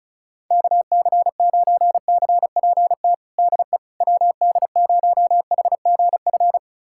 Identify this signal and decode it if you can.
Morse code, heard as a keyed tone, about 34 wpm, 700 Hz. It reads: KC9CPT DE WD0HGF
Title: USA Callsign DE Callsign - 6 Characters - 35wpm